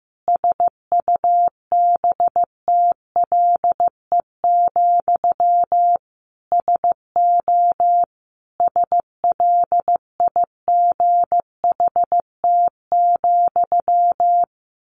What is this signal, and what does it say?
Morse code, heard as a keyed tone, about 15 wpm, 700 Hz. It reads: SUBTLE, SO SLIGHT,
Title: A Sound Of Thunder 15wpm